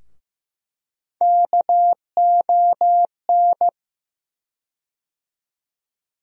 Morse code, heard as a keyed tone, about 15 wpm, 700 Hz. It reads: KON
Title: USA Call Sign Postfixes - 3 Letters 15wpm